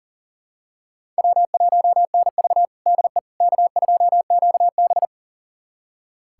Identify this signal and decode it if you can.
Morse code, heard as a keyed tone, about 40 wpm, 700 Hz. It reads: W1NV DE K2QB
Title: USA Callsign DE Callsign - 4 Characters - 40wpm